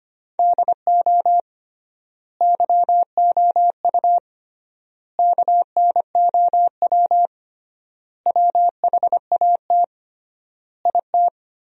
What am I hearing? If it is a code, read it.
Morse code, heard as a keyed tone, about 25 wpm, 700 Hz. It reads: DO YOU KNOW WHAT IT
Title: 3x Word-Spacing - Sentences from Top 100 Words - 25wpm